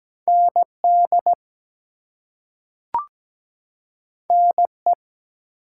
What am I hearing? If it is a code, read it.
Morse code, heard as a keyed tone, about 17 wpm, 700 Hz. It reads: ND NE
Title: US States and Canadian Provinces 17wpm